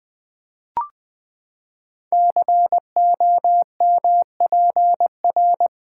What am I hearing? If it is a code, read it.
Morse code, heard as a keyed tone, about 20 wpm, 700 Hz. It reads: COMPR
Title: Build A Word - 5 Letters At A Time - 10-Letter Words - 20wpm